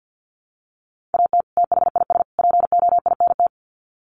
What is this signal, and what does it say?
Morse code, heard as a keyed tone, about 48 wpm, 700 Hz. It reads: AT THIS POINT
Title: Sets of 3 Words 50wpm